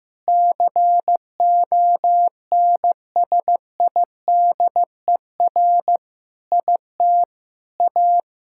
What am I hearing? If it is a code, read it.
Morse code, heard as a keyed tone, about 15 wpm, 700 Hz. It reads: CONSIDER IT A
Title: Sentences from Top 1000 Words - Review Part 6 - 15wpm